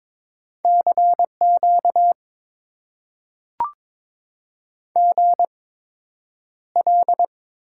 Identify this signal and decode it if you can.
Morse code, heard as a keyed tone, about 22 wpm, spaced as for 9 wpm, 700 Hz. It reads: CQ GL
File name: Top 90 QSO Elements - 8x Character Spacing - 22wpm